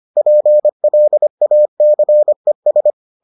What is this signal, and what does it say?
Morse code, heard as a keyed tone, about 25 wpm, 600 Hz. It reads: PLACES